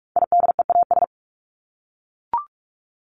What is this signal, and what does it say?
Morse code, heard as a keyed tone, about 44 wpm, 700 Hz. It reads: IDEAS
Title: ES:S Suffix - Top 100 Words with Base Words 45wpm